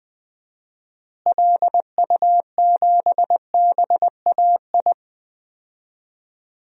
Morse code, heard as a keyed tone, about 20 wpm, 700 Hz. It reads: LU7BAI
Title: DXCC-100 Call Signs - 6 Characters - Repeated 2x - 20wpm